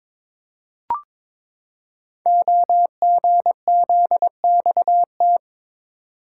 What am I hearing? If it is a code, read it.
Morse code, heard as a keyed tone, about 22 wpm, 700 Hz. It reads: OGZXT
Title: Code Groups - 5 Letters - 22wpm